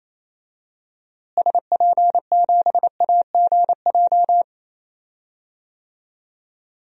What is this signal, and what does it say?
Morse code, heard as a keyed tone, about 28 wpm, 700 Hz. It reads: SP7AGJ
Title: DXCC-100 Call Signs - 6 Characters - Repeated 2x - 28wpm